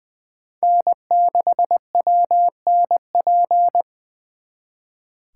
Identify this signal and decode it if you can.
Morse code, heard as a keyed tone, about 24 wpm, 700 Hz. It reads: N6WNP
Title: Speed Racing - USA 1x3 Call Signs 20wpm